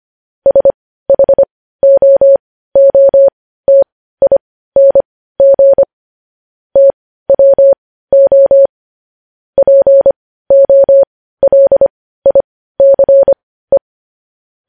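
Morse code, heard as a keyed tone, about 25 wpm, spaced as for 15 wpm, 550 Hz. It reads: SHOOTING TWO POLICE
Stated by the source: Morse code news headlines sent at 15 wpm.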